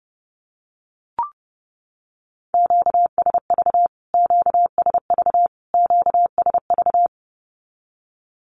Morse code, heard as a keyed tone, about 30 wpm, 700 Hz. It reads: QSV QSV QSV
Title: Q-Codes Repeated 3x 30wpm